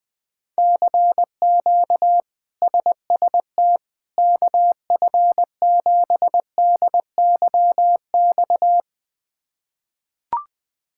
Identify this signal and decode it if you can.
Morse code, heard as a keyed tone, about 20 wpm, 700 Hz. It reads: CQ SST KF7DYX
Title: K1USN SST QSO Practice - 20wpm